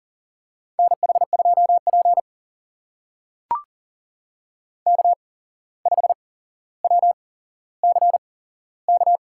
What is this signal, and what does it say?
Morse code, heard as a keyed tone, about 37 wpm, spaced as for 16 wpm, 700 Hz. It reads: NH2P K5WCK
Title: USA Call Signs - 8x Character Spacing - 40wpm